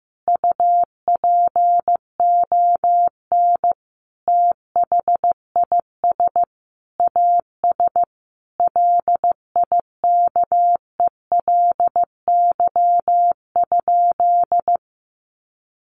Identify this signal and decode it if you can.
Morse code, heard as a keyed tone, about 15 wpm, 700 Hz. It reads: UPON THIS AS LIKELY?
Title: Sentences from Top 1100 Words 15wpm